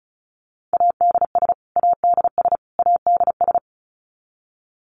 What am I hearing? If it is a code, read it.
Morse code, heard as a keyed tone, about 34 wpm, 700 Hz. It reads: ADS ADS ADS
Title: Abbreviations Repeated 3x - 35wpm